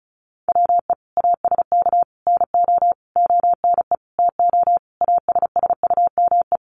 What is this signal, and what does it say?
Morse code, heard as a keyed tone, about 34 wpm, 700 Hz. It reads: WE ASK NO ONE TO ASSUME
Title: Sentences from Top 1500 Words 35wpm